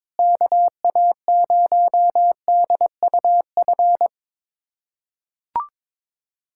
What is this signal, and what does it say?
Morse code, heard as a keyed tone, about 22 wpm, 700 Hz. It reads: KA0DUF
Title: USA Callsign DE Callsign 22wpm